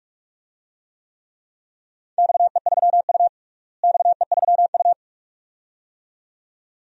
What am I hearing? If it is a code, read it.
Morse code, heard as a keyed tone, about 46 wpm, 700 Hz. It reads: XE3U XE3U
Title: DXCC-100 Call Signs - 4 Characters - Repeated 2x - 45wpm